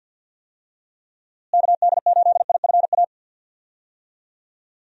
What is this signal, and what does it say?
Morse code, heard as a keyed tone, about 50 wpm, 700 Hz. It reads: KD8IFA
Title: USA Call Signs - 6 Characters - Repeated 3x - 50wpm